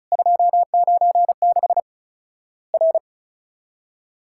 Morse code, heard as a keyed tone, about 36 wpm, 700 Hz. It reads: 296 R
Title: Worked All Europe DX Contest - QTC Exchange - 35wpm